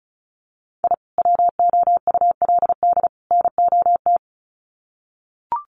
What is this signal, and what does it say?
Morse code, heard as a keyed tone, about 34 wpm, 700 Hz. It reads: I WOULD NOT
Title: Sets of 3 Words 35wpm